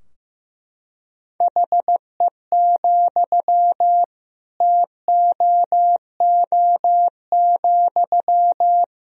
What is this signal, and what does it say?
Morse code, heard as a keyed tone, about 15 wpm, 700 Hz. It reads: HE, TOO,
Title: Sentences from Top 1000 Words - Review Part 3 - 15wpm